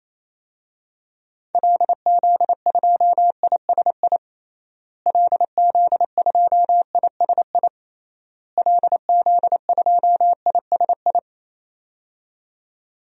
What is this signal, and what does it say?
Morse code, heard as a keyed tone, about 28 wpm, 700 Hz. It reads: LZ2ISI LZ2ISI LZ2ISI
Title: DXCC-100 Call Signs - 6 Characters - Repeated 3x - 28wpm